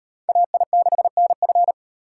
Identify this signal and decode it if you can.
Morse code, heard as a keyed tone, about 38 wpm, 700 Hz. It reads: AI6NF